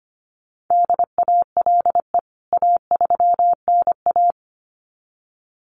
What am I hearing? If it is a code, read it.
Morse code, heard as a keyed tone, about 25 wpm, 700 Hz. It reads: DALE A3NA
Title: CWT Exchange 25wpm